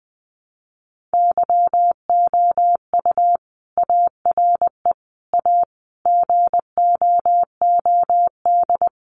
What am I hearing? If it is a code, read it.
Morse code, heard as a keyed tone, about 20 wpm, 700 Hz. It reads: YOU ARE A GOOD
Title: Sentences from Top 400 Words 20wpm